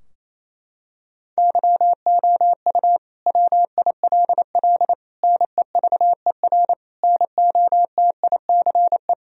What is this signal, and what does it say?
Morse code, heard as a keyed tone, about 28 wpm, 700 Hz. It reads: YOU WILL NEVER NOTICE
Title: Sentences from Top 1000 Words - Review Part 2 - 28wpm